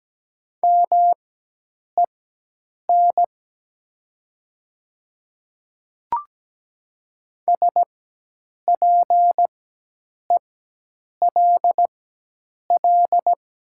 Morse code, heard as a keyed tone, about 17 wpm, spaced as for 8 wpm, 700 Hz. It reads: MEN SPELL
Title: 4x Character Spacing - Top 200 Words - 17wpm